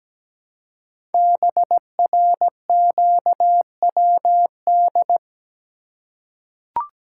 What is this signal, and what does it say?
Morse code, heard as a keyed tone, about 17 wpm, 700 Hz. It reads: BRQWD
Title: Code Groups - 5 Letters - 17wpm